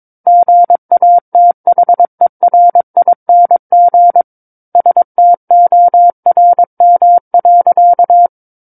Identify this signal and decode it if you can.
Morse code, heard as a keyed tone, about 22 wpm, 700 Hz. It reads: GATHERING STORM.
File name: Book One / 13-Chapter 13 - 22 WPM